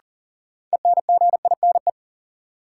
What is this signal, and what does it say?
Morse code, heard as a keyed tone, about 39 wpm, 700 Hz. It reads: ENGINE
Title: Sentences from Top 1500 Words - New Word Drill Alt - 40wpm